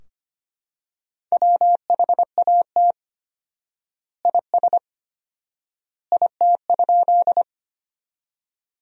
Morse code, heard as a keyed tone, about 25 wpm, 700 Hz. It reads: WHAT IS IT?
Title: 4x Word-Spacing - Sentences from Top 100 Words - 25wpm